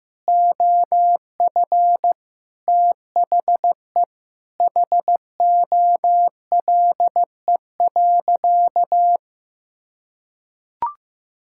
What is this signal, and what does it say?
Morse code, heard as a keyed tone, about 15 wpm, 700 Hz. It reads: OF THE HOLE.
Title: Sentences from Top 1600 Words 15wpm